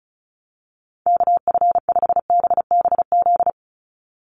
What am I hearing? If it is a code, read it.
Morse code, heard as a keyed tone, about 34 wpm, 700 Hz. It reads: KF5BBZ
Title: USA Call Signs 35wpm